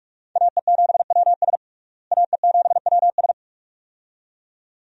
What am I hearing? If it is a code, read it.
Morse code, heard as a keyed tone, about 41 wpm, 700 Hz. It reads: AE7WS AE7WS
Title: USA Call Signs - 5 Characters - Repeated 2x - 45wpm